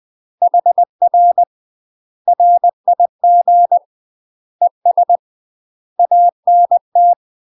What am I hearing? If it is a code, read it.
Morse code, heard as a keyed tone, about 20 wpm, 700 Hz. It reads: HR RIG ES ANT
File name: UK MASTER QSOs TRAINING / 20WPM / UK MASTER QSOs 0023